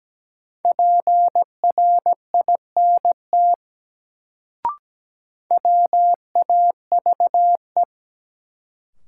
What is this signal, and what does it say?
Morse code, heard as a keyed tone, about 17 wpm, 700 Hz. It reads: PRINT WAVE
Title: Sentences from Top 1400 Words - New Word Drill - 17wpm